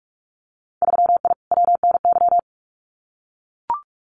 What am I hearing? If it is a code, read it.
Morse code, heard as a keyed tone, about 44 wpm, 700 Hz. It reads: 3I WNY
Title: Winter Field Day Exchanges 45wpm